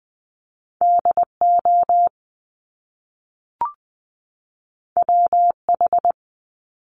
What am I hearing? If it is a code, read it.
Morse code, heard as a keyed tone, about 20 wpm, 700 Hz. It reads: D O W H
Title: Top Two Letter Combinations in Words 20wpm